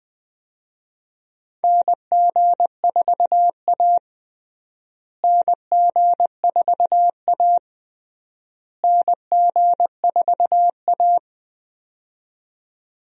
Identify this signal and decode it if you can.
Morse code, heard as a keyed tone, about 20 wpm, 700 Hz. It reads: NG4A NG4A NG4A
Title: USA Call Signs - 4 Characters - Repeated 3x - 20wpm